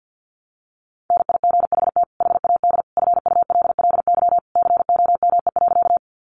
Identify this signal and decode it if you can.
Morse code, heard as a keyed tone, about 48 wpm, 700 Hz. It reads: NIGHT HAD FULLY COME.
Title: Sentences from Top 1300 Words 50wpm